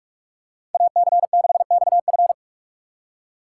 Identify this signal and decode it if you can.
Morse code, heard as a keyed tone, about 48 wpm, 700 Hz. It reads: AC6XF
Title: USA Call Signs - 4x Character Spacing - 45wpm